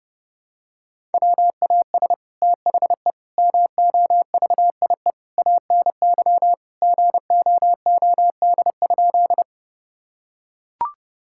Morse code, heard as a keyed tone, about 30 wpm, 700 Hz. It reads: WAS THE MOVIE ANY GOOD?
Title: Sentences from Top 1000 Words - Review Part 5 - 30wpm